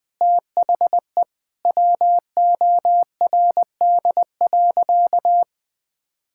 Morse code, heard as a keyed tone, about 20 wpm, 700 Hz. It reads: THE WORD.
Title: Sentences with 6-Word Phrases - Spoken Beginning - 20wpm